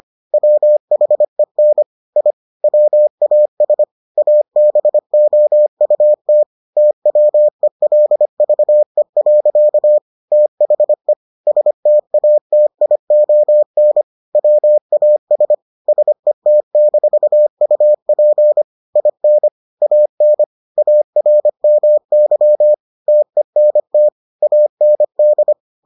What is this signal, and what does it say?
Morse code, heard as a keyed tone, about 25 wpm, 600 Hz. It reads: WHEN I WAS ABOUT TWELVE. THE STATION WAS SET-UP IN AN ARMY TENT AND